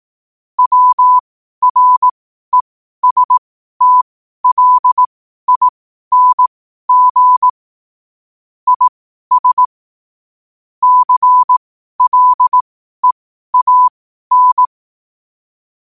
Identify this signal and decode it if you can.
Morse code, heard as a keyed tone, about 18 wpm, spaced as for 12 wpm, 1000 Hz. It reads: WRESTLING IS CLEAN
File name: 12wpm